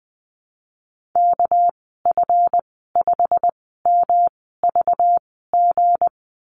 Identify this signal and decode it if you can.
Morse code, heard as a keyed tone, about 19 wpm, spaced as for 14 wpm, 700 Hz. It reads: KF5MVG
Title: 200 Lesson 32g - Letter k - Call Signs 20wpm